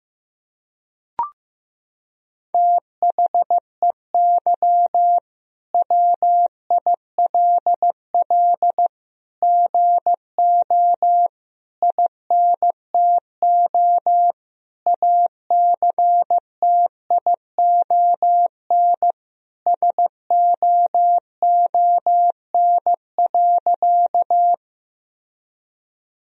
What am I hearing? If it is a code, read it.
Morse code, heard as a keyed tone, about 16 wpm, 700 Hz. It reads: THEY WILL GO INTO ACTION SOON.
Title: Sentences from Top 1000 Words - Review Part 4 - 15wpm